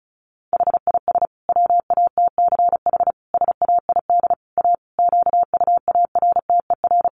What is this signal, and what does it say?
Morse code, heard as a keyed tone, about 34 wpm, 700 Hz. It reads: HIS WATCH SAID A QUARTER
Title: Sentences from Top 1300 Words 35wpm